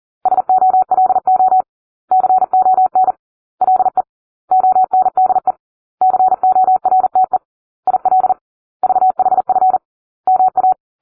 Morse code, heard as a keyed tone, about 40 wpm, 750 Hz. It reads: SOLO CON LE ONDE CORTE IL VHF MA